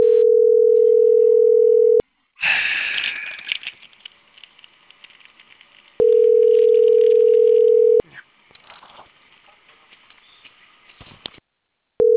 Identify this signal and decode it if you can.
Ringback tone (North America), 440 + 480 Hz.